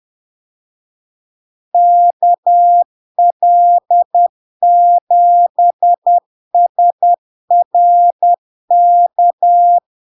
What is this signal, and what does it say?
Morse code, heard as a keyed tone, about 10 wpm, 700 Hz. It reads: KL7SRK
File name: LARK-FD-2022-Log-in-CW-10wpm-700-Hz